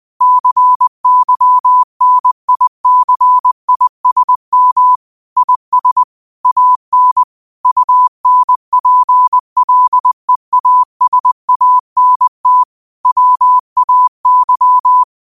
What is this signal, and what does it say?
Morse code, heard as a keyed tone, about 20 wpm, 1000 Hz. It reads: CYNICISM IS AN UNPLEASANT WAY